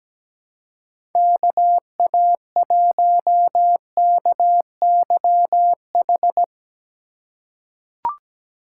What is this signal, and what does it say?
Morse code, heard as a keyed tone, about 17 wpm, 700 Hz. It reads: KA1KYH
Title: USA Call Signs - 6 Characters - 4x Character Spacing - 17wpm